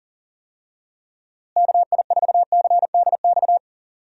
Morse code, heard as a keyed tone, about 40 wpm, 700 Hz. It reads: KI4CDX
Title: USA Call Signs - 6 Characters - Repeated 2x - 40wpm